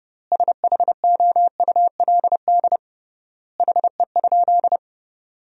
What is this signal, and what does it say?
Morse code, heard as a keyed tone, about 30 wpm, 700 Hz. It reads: SHOULD HE?
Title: 3x Word-Spacing - Sentences from Top 300 Words - 30wpm